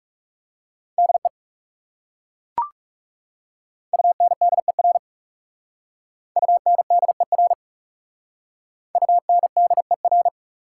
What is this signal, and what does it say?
Morse code, heard as a keyed tone, about 40 wpm, 690 Hz. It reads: DE UNDER UNDER UNDER